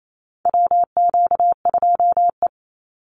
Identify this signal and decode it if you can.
Morse code, heard as a keyed tone, about 28 wpm, 700 Hz. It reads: WQ2E
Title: Call Signs Repeated 3x - 28wpm